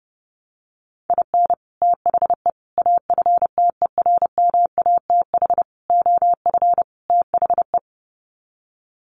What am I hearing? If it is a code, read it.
Morse code, heard as a keyed tone, about 30 wpm, 700 Hz. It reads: IN THE AFTERMATH OF THE
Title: Sets of 5 Words 30wpm